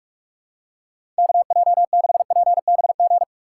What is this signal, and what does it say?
Morse code, heard as a keyed tone, about 46 wpm, 700 Hz. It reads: KJ6PBG
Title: USA Call Signs - 3x Character Spacing - 45wpm